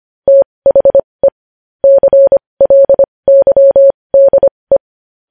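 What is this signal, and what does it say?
Morse code, heard as a keyed tone, about 25 wpm, spaced as for 20 wpm, 550 Hz. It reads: THE CLYDE
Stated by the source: Morse code news headlines practice at 20 wpm.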